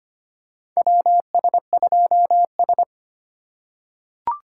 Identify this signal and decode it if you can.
Morse code, heard as a keyed tone, about 25 wpm, 700 Hz. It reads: WS2S
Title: USA Call Signs - 4 Characters - 4x Character Spacing - 25wpm